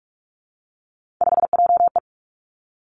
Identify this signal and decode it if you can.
Morse code, heard as a keyed tone, about 46 wpm, 700 Hz. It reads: 5JE
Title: Three Challenging Letter-Number Combinations 45wpm